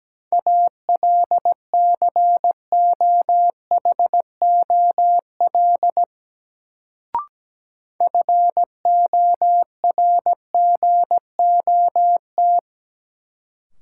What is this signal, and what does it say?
Morse code, heard as a keyed tone, about 17 wpm, 700 Hz. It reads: ALCOHOL FORGOT
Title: Sentences from Top 1400 Words - New Word Drill - 17wpm